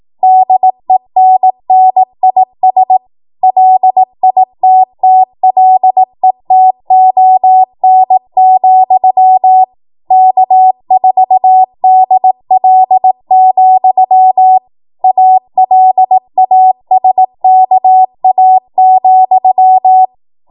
Morse code, heard as a keyed tone, about 18 wpm, 750 Hz. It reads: DENNIS LITTLETON, K4DL, ALASKA,
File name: Morse / 18WPM